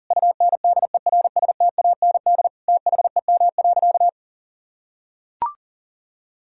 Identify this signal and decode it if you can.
Morse code, heard as a keyed tone, about 40 wpm, 700 Hz. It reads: UNDERSTAND THEM.
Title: Sentences from Top 1000 Words - Review Part 2 - 40wpm